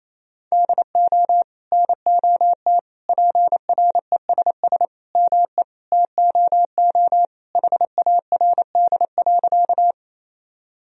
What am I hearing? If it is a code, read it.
Morse code, heard as a keyed tone, about 28 wpm, 700 Hz. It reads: DO NOT PRESS ME TOO HARD.
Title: Sentences from Top 1000 Words - Review Part 1 - 28wpm